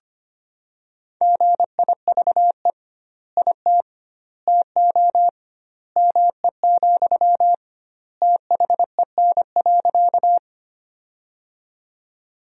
Morse code, heard as a keyed tone, about 25 wpm, 700 Hz. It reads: GIVE IT TO ME, THEN.
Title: 2x Word-Spacing - Sentences from Top 200 Words - 25wpm